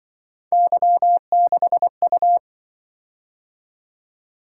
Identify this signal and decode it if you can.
Morse code, heard as a keyed tone, about 25 wpm, 700 Hz. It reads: Y6U